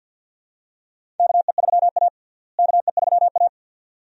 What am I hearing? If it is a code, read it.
Morse code, heard as a keyed tone, about 47 wpm, 700 Hz. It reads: KE3A KE3A
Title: USA Call Signs - 4 Characters - Repeated 2x - 50wpm